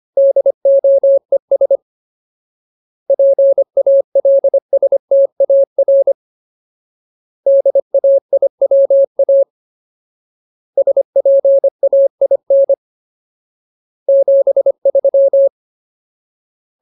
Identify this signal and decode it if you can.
Morse code, heard as a keyed tone, about 25 wpm, 550 Hz. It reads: DOES PALSTAR DAIWA SPAIN 73
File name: Ultimate Word List / 25WPM Ultimate